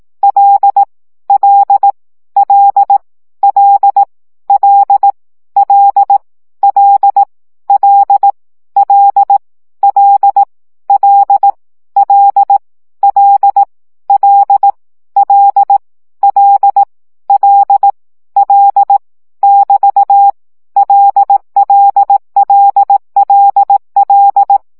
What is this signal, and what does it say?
Morse code, heard as a keyed tone, about 18 wpm, 800 Hz. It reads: L L L L L L L L L L L L L L L L L L <BT> LLLLL
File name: CW gyakorlas l 15 baud